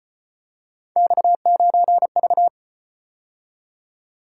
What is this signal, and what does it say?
Morse code, heard as a keyed tone, about 36 wpm, 700 Hz. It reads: X9V